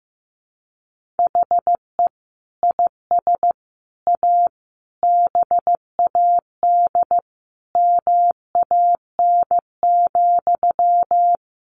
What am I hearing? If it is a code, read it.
Morse code, heard as a keyed tone, about 15 wpm, 700 Hz. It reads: HE IS A BAD MAN,
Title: Sentences from Top 1100 Words 15wpm